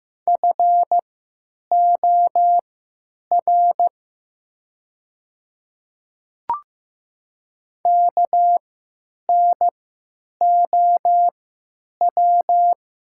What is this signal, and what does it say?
Morse code, heard as a keyed tone, about 15 wpm, spaced as for 9 wpm, 700 Hz. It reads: FOR KNOW
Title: 3x Character Spacing - Top 100 Words - 15wpm